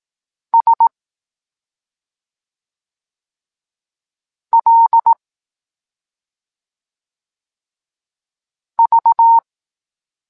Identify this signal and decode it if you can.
Morse code, heard as a keyed tone, about 18 wpm, 920 Hz. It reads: S L V
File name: Tekst-26 1